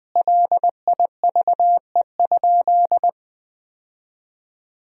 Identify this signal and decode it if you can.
Morse code, heard as a keyed tone, about 21 wpm, 700 Hz. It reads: LIVE?
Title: 4x Word-Spacing - Sentences from Top 200 Words - 20wpm